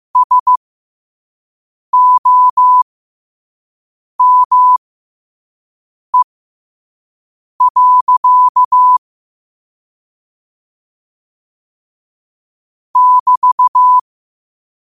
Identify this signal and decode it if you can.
Morse code, heard as a keyed tone, about 15 wpm, spaced as for 5 wpm, 1000 Hz. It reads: SOME. <BT>
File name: qotdfiles / 5wpm